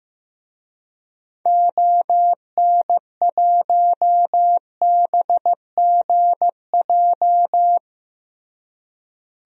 Morse code, heard as a keyed tone, about 15 wpm, 700 Hz. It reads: ON1BGJ
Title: DXCC-100 Call Signs - 6 Characters - Repeated 2x - 15wpm